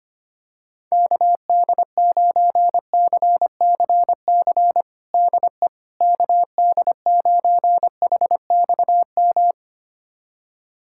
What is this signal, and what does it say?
Morse code, heard as a keyed tone, about 25 wpm, 700 Hz. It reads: KD9CCC DE KD9HXM
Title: USA Callsign DE Callsign - 6 Characters - 25wpm